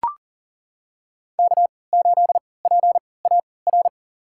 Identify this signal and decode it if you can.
Morse code, heard as a keyed tone, about 40 wpm, spaced as for 23 wpm, 700 Hz. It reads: K8PAR
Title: USA Call Signs - 5 Characters - 3x Character Spacing - 40wpm